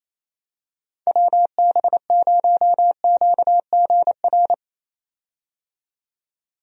Morse code, heard as a keyed tone, about 28 wpm, 700 Hz. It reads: WB0QGR
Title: USA Call Signs - 6 Characters - 28wpm